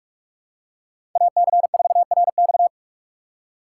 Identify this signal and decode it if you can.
Morse code, heard as a keyed tone, about 47 wpm, 700 Hz. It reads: AC4RX
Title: USA Call Signs - 5 Characters - 6x Character Spacing - 45wpm